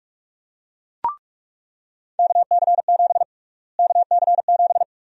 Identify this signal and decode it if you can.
Morse code, heard as a keyed tone, about 42 wpm, 700 Hz. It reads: KC7 KC7
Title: USA Call Sign Prefixes - Repeated 2x - 45wpm